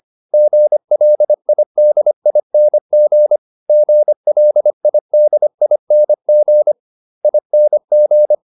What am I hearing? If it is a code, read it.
Morse code, heard as a keyed tone, about 25 wpm, 600 Hz. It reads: GLIDING GLIDING ING